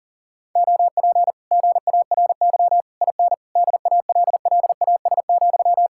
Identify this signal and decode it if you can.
Morse code, heard as a keyed tone, about 35 wpm, 700 Hz. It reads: OP GARY IN DALLAS,
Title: Speed-Racing Common QSO Phrases Encore - 40wpm